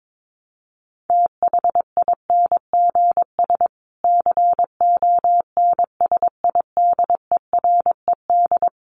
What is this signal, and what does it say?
Morse code, heard as a keyed tone, about 22 wpm, 700 Hz. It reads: THINGS CONSIDERED
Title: Sets of 5 Words Encore 22wpm